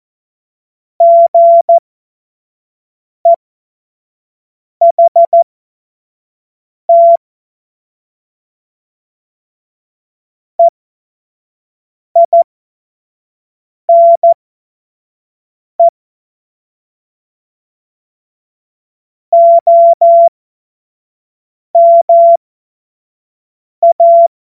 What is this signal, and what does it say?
Morse code, heard as a keyed tone, about 14 wpm, spaced as for 5 wpm, 680 Hz. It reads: GEHT EINE OMA
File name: Morsekurs / Witze / Witz 12. Oktober 2021, 25 F70